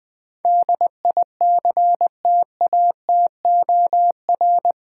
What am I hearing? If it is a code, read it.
Morse code, heard as a keyed tone, about 20 wpm, 700 Hz. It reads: DICTATOR
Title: Build A Word - 4 Letters At A Time - 8-Letter Words - 20wpm